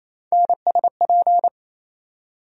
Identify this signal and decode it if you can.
Morse code, heard as a keyed tone, about 28 wpm, 700 Hz. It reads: NSP
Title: USA Call Sign Postfixes - 3 Letters 28wpm